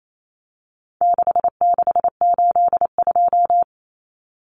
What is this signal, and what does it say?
Morse code, heard as a keyed tone, about 28 wpm, 700 Hz. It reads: 6682
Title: Numbers - 4-Digit 28wpm